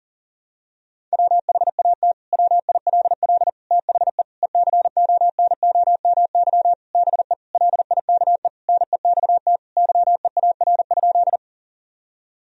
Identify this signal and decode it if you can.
Morse code, heard as a keyed tone, about 40 wpm, 700 Hz. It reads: WHAT WILL THE ECONOMY BE LIKE NEXT YEAR?
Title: Sentences from Top 1000 Words - Review Part 3 - 40wpm